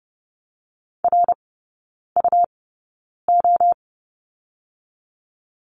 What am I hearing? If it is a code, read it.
Morse code, heard as a keyed tone, about 30 wpm, spaced as for 9 wpm, 700 Hz. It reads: RUO
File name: Lesson 17e - Letter u - Three Characters 30wpm